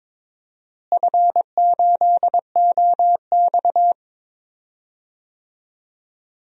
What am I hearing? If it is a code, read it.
Morse code, heard as a keyed tone, about 22 wpm, 700 Hz. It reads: F8OX
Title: DXCC-100 Call Signs - 4 Characters - Repeated 2x - 22wpm